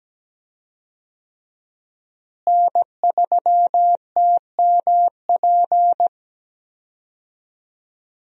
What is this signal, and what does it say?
Morse code, heard as a keyed tone, about 17 wpm, 700 Hz. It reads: N3TMP
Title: USA Call Signs - 5 Characters - Repeated 2x - 17wpm